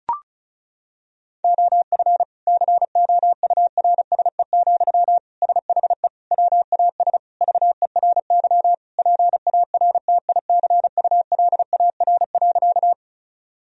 Morse code, heard as a keyed tone, about 36 wpm, 700 Hz. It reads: OF COURSE, SHE WAS VERY PARTICULAR.
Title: Sentences from Top 1000 Words - Review Part 4 - 35wpm